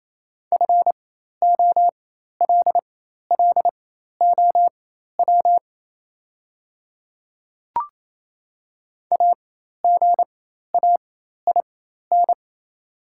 Morse code, heard as a keyed tone, about 29 wpm, spaced as for 13 wpm, 690 Hz. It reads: FOLLOW AGAIN
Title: 4x Character Spacing - Top 200 Words - 28wpm